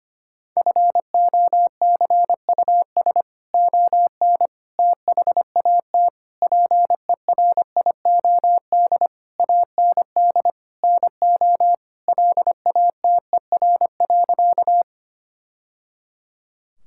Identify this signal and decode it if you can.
Morse code, heard as a keyed tone, about 25 wpm, 700 Hz. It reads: FOCUS ON THAT PERIOD AND NO LATER.
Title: Sentences from Top 1000 Words - Review Part 5 - 25wpm